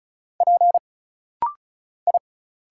Morse code, heard as a keyed tone, about 34 wpm, 700 Hz. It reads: P I
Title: Single Letters - Rapid-Fire 35wpm